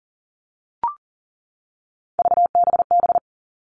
Morse code, heard as a keyed tone, about 38 wpm, 700 Hz. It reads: VBB
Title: Three Challenging Letter-Number Combinations 40wpm